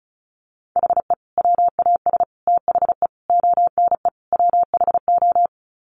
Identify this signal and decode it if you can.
Morse code, heard as a keyed tone, about 34 wpm, 700 Hz. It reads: HE WAS THE ONE WHO
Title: Sets of 5 Words Encore2 35wpm